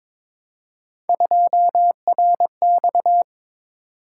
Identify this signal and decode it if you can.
Morse code, heard as a keyed tone, about 22 wpm, 700 Hz. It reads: 2RX
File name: practice / 22wpm / Three Letter-Number Combinations - 22wpm